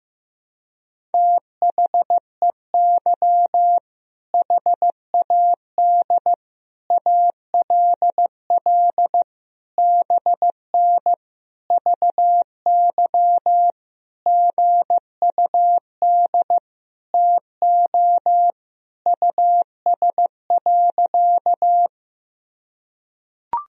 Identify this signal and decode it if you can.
Morse code, heard as a keyed tone, about 15 wpm, 700 Hz. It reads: THEY HAD ALL BN VY GUD TO US.
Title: Sentences from Top 200 Words and Abbreviations 15wpm